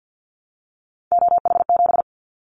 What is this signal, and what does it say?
Morse code, heard as a keyed tone, about 50 wpm, 700 Hz. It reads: OH7
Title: Callsign Prefixes for Top 50 DXCC - 50wpm